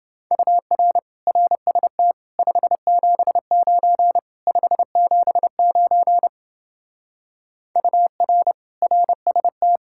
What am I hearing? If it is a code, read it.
Morse code, heard as a keyed tone, about 29 wpm, 700 Hz. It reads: UR RST 579 579 UR RST